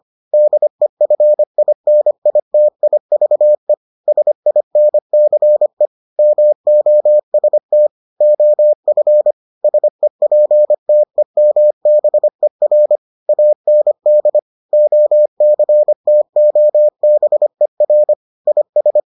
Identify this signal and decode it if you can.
Morse code, heard as a keyed tone, about 25 wpm, 600 Hz. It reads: DEFINITIVE SINCE MOST OF SEPTEMBER AND OCTOBER IS